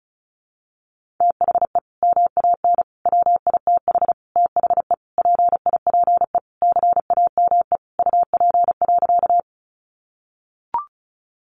Morse code, heard as a keyed tone, about 34 wpm, 700 Hz. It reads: THE MAN WITH THE PIPE CAME UP.
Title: Sentences from Top 1500 Words 35wpm